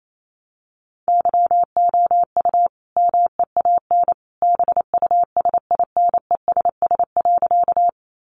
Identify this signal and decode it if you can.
Morse code, heard as a keyed tone, about 28 wpm, 700 Hz. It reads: YOU MEAN BUSINESS.
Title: Sentences from Top 600 Words 28wpm